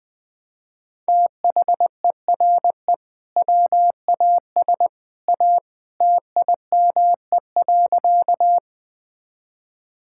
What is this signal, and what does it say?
Morse code, heard as a keyed tone, about 21 wpm, 700 Hz. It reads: THERE WAS A TIME.
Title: Sentences with 2-Word Phrases - 20wpm